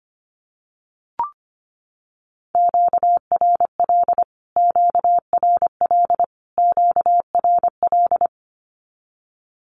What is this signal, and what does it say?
Morse code, heard as a keyed tone, about 25 wpm, 700 Hz. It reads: QRL QRL QRL
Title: Q-Codes Repeated 3x 25wpm